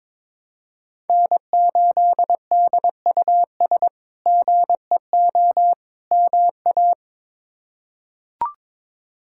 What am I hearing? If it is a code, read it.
Morse code, heard as a keyed tone, about 22 wpm, 700 Hz. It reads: N8DUS GEO MA
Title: K1USN SST QSO Practice - 22wpm